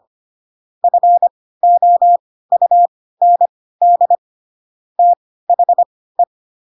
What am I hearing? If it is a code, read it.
Morse code, heard as a keyed tone, about 25 wpm, spaced as for 16 wpm, 700 Hz. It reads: FOUND THE
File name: Pinocchio / Pinocchio ch2 10 25 ik1hge